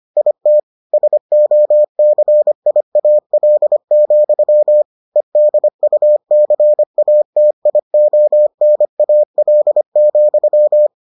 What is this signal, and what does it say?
Morse code, heard as a keyed tone, about 25 wpm, 600 Hz. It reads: IT SOCIAL, EDUCATIONAL,